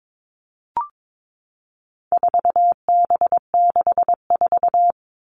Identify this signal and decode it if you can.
Morse code, heard as a keyed tone, about 22 wpm, 700 Hz. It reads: 4B64
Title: All Cut Numbers - 4-Digit 22wpm